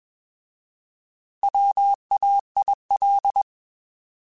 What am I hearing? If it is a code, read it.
Morse code, heard as a keyed tone, about 20 wpm, 800 Hz. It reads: WAIL
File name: Alphabetical / Alphabetical mp3s / Ws @20